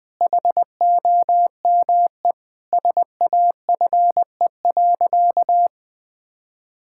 Morse code, heard as a keyed tone, about 21 wpm, 700 Hz. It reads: HOME SAFE.
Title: Sentences from Top 1600 Words 20wpm